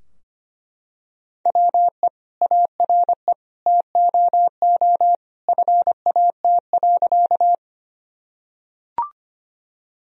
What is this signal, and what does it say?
Morse code, heard as a keyed tone, about 25 wpm, 700 Hz. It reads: WE ARE TOO FAT.
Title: Sentences from Top 1000 Words - Review Part 4 - 25wpm